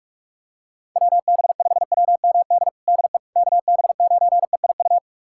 Morse code, heard as a keyed tone, about 42 wpm, 700 Hz. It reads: WB5WMD DE KB9EIU
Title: USA Callsign DE Callsign - 6 Characters - 45wpm